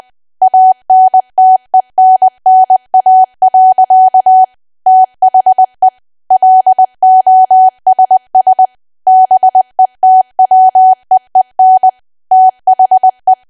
Morse code, heard as a keyed tone, about 20 wpm, 750 Hz. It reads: ANTENNA. THE LOSS BETWEEN THE
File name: cwa / W1AW-Level3 / 20WPM